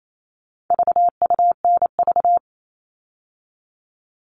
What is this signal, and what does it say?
Morse code, heard as a keyed tone, about 28 wpm, 700 Hz. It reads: VUNV
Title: All Cut Numbers - 4-Digit 28wpm